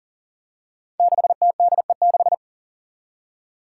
Morse code, heard as a keyed tone, about 40 wpm, 700 Hz. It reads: 6TDE6
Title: Five Challenging Letter-Number Combinations 40wpm